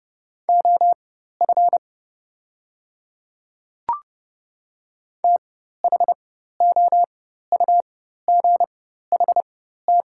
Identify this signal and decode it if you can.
Morse code, heard as a keyed tone, about 30 wpm, spaced as for 14 wpm, 700 Hz. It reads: OF THOUGHT
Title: 4x Character Spacing - Top 300 Words - 30wpm